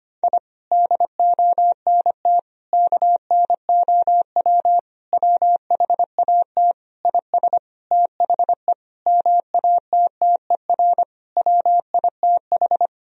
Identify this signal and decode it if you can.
Morse code, heard as a keyed tone, about 25 wpm, 700 Hz. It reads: I DONT KNOW WHAT IS THE MATTER WITH
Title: Sentences from Top 1000 Words - Review Part 4 - 25wpm